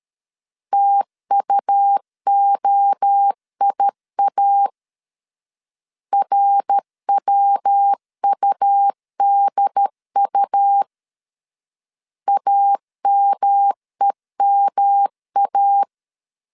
Morse code, heard as a keyed tone, about 13 wpm, 800 Hz. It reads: TUOIA RWUDU AMEMA
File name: CW / DET / 1 3 1500C S B 12wpm 2828